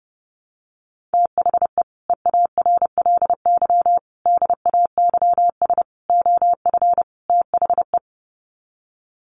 Morse code, heard as a keyed tone, about 30 wpm, 700 Hz. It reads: THE EARLY DAYS OF THE
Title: Sets of 5 Words Encore 30wpm